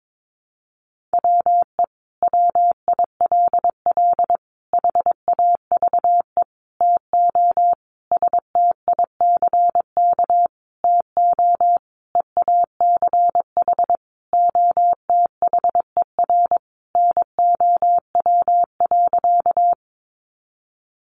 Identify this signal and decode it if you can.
Morse code, heard as a keyed tone, about 22 wpm, 700 Hz. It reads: WE WILL HAVE TO STICK TO EACH OTHER NOW.
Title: Sentences from Top 1200 Words 22wpm